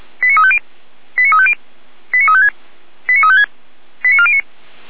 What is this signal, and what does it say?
Five-tone selective call (ZVEI), heard as quick runs of five tones, tones of about 70 ms. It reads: 89350 89350 89356 89356 89409